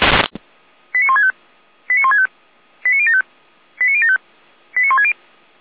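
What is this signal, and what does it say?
Five-tone selective call (ZVEI), heard as quick runs of five tones, tones of about 80 ms. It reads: 89165 89165 89075 89075 89160
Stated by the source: ZVEI five-tone call 89170 received on SDR.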